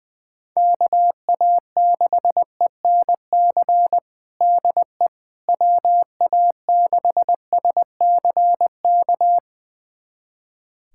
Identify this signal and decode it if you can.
Morse code, heard as a keyed tone, about 20 wpm, 700 Hz. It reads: KA6ENC DE WA6SCK
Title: USA Callsign DE Callsign - 6 Characters - 20wpm